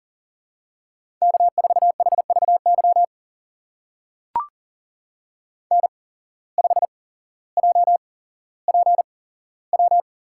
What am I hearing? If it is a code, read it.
Morse code, heard as a keyed tone, about 40 wpm, spaced as for 17 wpm, 700 Hz. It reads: K4HVY N5JPW
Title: USA Call Signs - 5 Characters - 8x Character Spacing - 40wpm